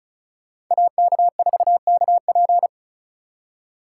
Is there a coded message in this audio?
Yes: Morse code from 0.7 to 2.7 s, heard as a keyed tone, about 45 wpm, 700 Hz. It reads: AK4KP